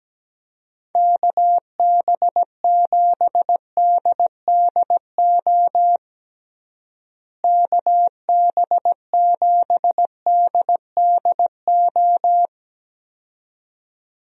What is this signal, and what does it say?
Morse code, heard as a keyed tone, about 17 wpm, 700 Hz. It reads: KB7DDO KB7DDO
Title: USA Call Signs - 6 Characters - Repeated 2x - 17wpm